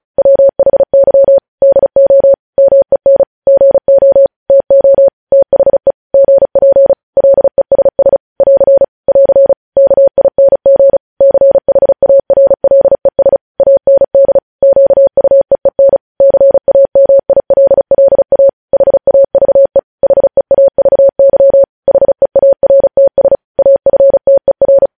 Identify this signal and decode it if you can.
Morse code, heard as a keyed tone, about 36 wpm, 550 Hz. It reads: WHY DO MEN GO TO THE GP LESS <AR> <AR> KING CHARLES AND QUEEN CAMILLA HAVE HEAVY HEARTS AFTER
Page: News Headlines in Morse Code at 35 WPM